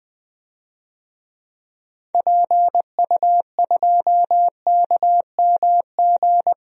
Morse code, heard as a keyed tone, about 20 wpm, 700 Hz. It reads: PU2KMG
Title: DXCC-100 Call Signs - 6 Characters - 20wpm